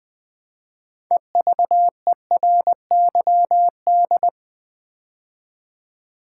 Morse code, heard as a keyed tone, about 20 wpm, 700 Hz. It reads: EVERYD
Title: Build A Word - 2 Letters At A Time - 8-Letter Words - 20wpm